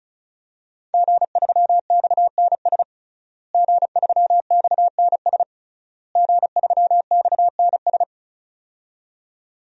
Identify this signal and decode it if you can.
Morse code, heard as a keyed tone, about 36 wpm, 700 Hz. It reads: G3XNS G3XNS G3XNS
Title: DXCC-100 Call Signs - 5 Characters - Repeated 3x - 35wpm